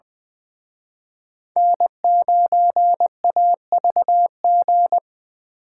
Morse code, heard as a keyed tone, about 24 wpm, 700 Hz. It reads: N9AVG
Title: Speed Racing - USA 1x3 Call Signs 20wpm